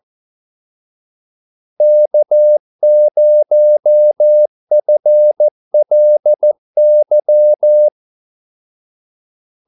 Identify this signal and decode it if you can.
Morse code, heard as a keyed tone, about 14 wpm, 600 Hz. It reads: K0FLY
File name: Callsigns-14wpm